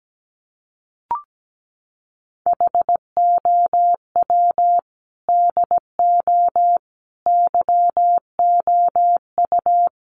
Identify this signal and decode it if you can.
Morse code, heard as a keyed tone, about 17 wpm, 700 Hz. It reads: HOW DO YOU
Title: Sentences from Top 1100 Words 17wpm